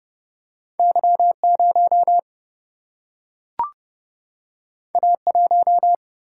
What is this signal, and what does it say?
Morse code, heard as a keyed tone, about 30 wpm, 700 Hz. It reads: Y 0 A 1
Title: Single-Letter Single-Number 30wpm